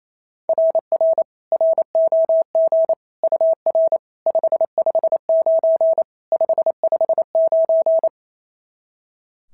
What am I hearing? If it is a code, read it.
Morse code, heard as a keyed tone, about 28 wpm, 690 Hz. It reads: RR ROG UR 559 559
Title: SOTA QSO Practice 28wpm